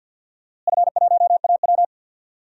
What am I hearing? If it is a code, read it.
Morse code, heard as a keyed tone, about 51 wpm, 700 Hz. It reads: F1AW
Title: DXCC-100 Call Signs - 4 Characters - Repeated 2x - 50wpm